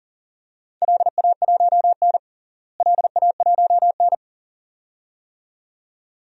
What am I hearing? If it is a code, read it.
Morse code, heard as a keyed tone, about 40 wpm, 700 Hz. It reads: LA1N LA1N
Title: DXCC-100 Call Signs - 4 Characters - Repeated 2x - 40wpm